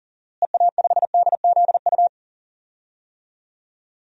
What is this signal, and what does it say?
Morse code, heard as a keyed tone, about 40 wpm, 700 Hz. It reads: EA5DZU
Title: DXCC-100 Call Signs - 6 Characters - Repeated 3x - 40wpm